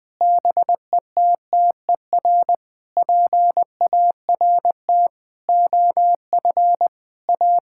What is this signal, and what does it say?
Morse code, heard as a keyed tone, about 20 wpm, 700 Hz. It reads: BETTER PART OF A
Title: Sentences with 7-Word Phrases - Spoken Beginning - 20wpm